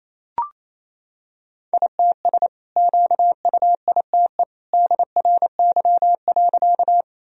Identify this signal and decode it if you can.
Morse code, heard as a keyed tone, about 28 wpm, 700 Hz. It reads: ITS QUITE DRY.
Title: Sentences from Top 1000 Words - Review Part 3 - 28wpm